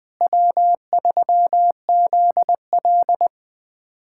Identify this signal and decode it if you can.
Morse code, heard as a keyed tone, about 20 wpm, 700 Hz. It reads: W3ZL
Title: USA Callsign DE Callsign - 4 Characters - 20wpm